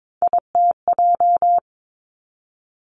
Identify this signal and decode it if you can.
Morse code, heard as a keyed tone, about 22 wpm, 700 Hz. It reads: ITJ
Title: Three Challenging Letter-Number Combinations 22wpm